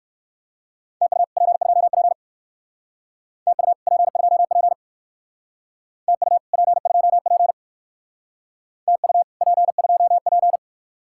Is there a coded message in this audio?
Yes: Morse code from 1.0 to 10.6 s, heard as a keyed tone, about 58 wpm, 700 Hz. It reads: TU P2P TU P2P TU P2P TU P2P